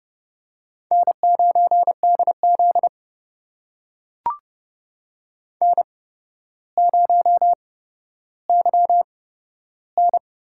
Morse code, heard as a keyed tone, about 30 wpm, spaced as for 13 wpm, 700 Hz. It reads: N9DZ N0YN
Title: USA Call Signs - 4 Characters - 8x Character Spacing - 30wpm